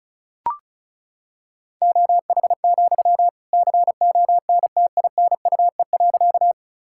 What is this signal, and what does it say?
Morse code, heard as a keyed tone, about 34 wpm, 700 Hz. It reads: OH, CONTINUE.
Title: Sentences from Top 1000 Words - Review Part 6 - 35wpm